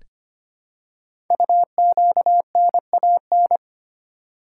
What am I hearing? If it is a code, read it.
Morse code, heard as a keyed tone, about 25 wpm, 700 Hz. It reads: UQNAN
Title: Code Groups - 5 Letters - 25wpm